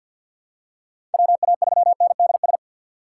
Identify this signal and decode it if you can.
Morse code, heard as a keyed tone, about 53 wpm, 700 Hz. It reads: WA3NDS
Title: USA Call Signs - 6 Characters - 3x Character Spacing - 50wpm